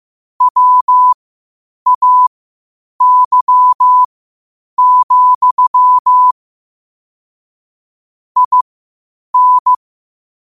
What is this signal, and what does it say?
Morse code, heard as a keyed tone, about 15 wpm, spaced as for 8 wpm, 1000 Hz. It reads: WAY, IN
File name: qotdfiles / 8wpm